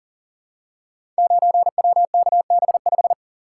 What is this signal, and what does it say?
Morse code, heard as a keyed tone, about 40 wpm, 700 Hz. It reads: 9WKB5
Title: Code Groups - 5 Letter:Numbers - 40wpm